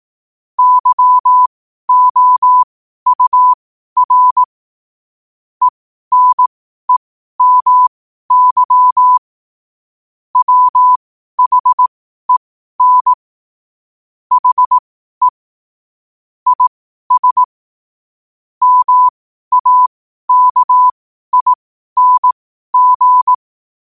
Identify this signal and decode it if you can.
Morse code, heard as a keyed tone, about 18 wpm, spaced as for 12 wpm, 1000 Hz. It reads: YOUR ENEMY WHEN HE IS MAKING
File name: qotdfiles / 12wpm